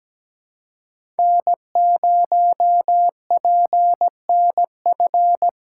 Morse code, heard as a keyed tone, about 17 wpm, 700 Hz. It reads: N0PNF
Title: USA Call Signs - 5 Characters - 4x Character Spacing - 17wpm